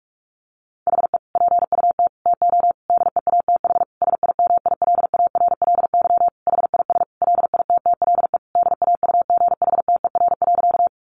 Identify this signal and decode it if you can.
Morse code, heard as a keyed tone, about 44 wpm, 700 Hz. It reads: HE PUT TO DEATH SIMILARLY HIS LITTLE DAUGHTER.
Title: Sentences from Top 1500 Words 45wpm